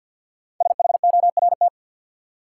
Morse code, heard as a keyed tone, about 48 wpm, 700 Hz. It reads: SHORT
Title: Sentences from Top 400 Words - New Word Drill - 50wpm